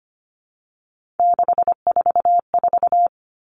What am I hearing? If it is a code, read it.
Morse code, heard as a keyed tone, about 25 wpm, 700 Hz. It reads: 644
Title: All Cut Numbers - 3-Digit 25wpm